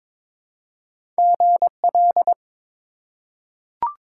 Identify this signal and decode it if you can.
Morse code, heard as a keyed tone, about 21 wpm, 700 Hz. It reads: GL
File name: Top 90 QSO Elements - 2x Character Spacing - 22wpm